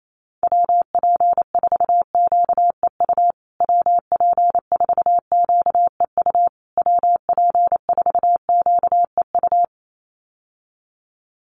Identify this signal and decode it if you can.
Morse code, heard as a keyed tone, about 28 wpm, 700 Hz. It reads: WP4QEU WP4QEU WP4QEU
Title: USA Call Signs Repeated 3x 28wpm